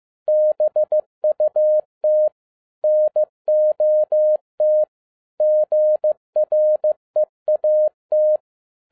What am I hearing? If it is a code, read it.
Morse code, heard as a keyed tone, about 15 wpm, 610 Hz. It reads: BUT NOT GREAT
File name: 2011-07-mp3 / n07 A LOT HAS CHANGED 15wpm